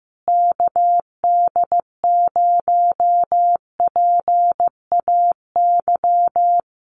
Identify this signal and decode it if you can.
Morse code, heard as a keyed tone, about 15 wpm, 700 Hz. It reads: KD0PAY
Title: Common QSO Phrases Encore 15wpm